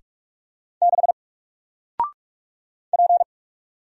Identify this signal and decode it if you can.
Morse code, heard as a keyed tone, about 44 wpm, 700 Hz. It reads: 6 P
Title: Single Letter-Number 45wpm